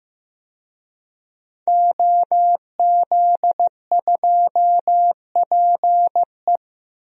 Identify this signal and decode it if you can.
Morse code, heard as a keyed tone, about 15 wpm, 700 Hz. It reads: OZ2PE
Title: DXCC-100 Call Signs - 5 Characters - 15wpm